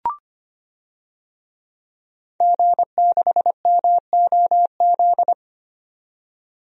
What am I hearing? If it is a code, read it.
Morse code, heard as a keyed tone, about 25 wpm, 700 Hz. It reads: G6MOZ
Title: DXCC-100 Call Signs - 5 Characters - 25wpm